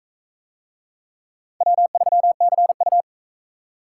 Morse code, heard as a keyed tone, about 38 wpm, 700 Hz. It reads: W3CU